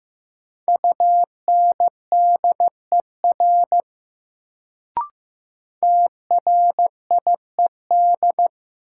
Morse code, heard as a keyed tone, about 15 wpm, 700 Hz. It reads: UNDER TRIED
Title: Sets of 2 Words - New Word Drill - 15wpm